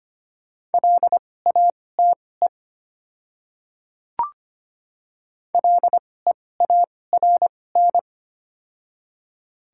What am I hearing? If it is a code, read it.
Morse code, heard as a keyed tone, about 25 wpm, spaced as for 18 wpm, 700 Hz. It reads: LATE LEARN
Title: 2x Character Spacing - Top 500 Words - 25wpm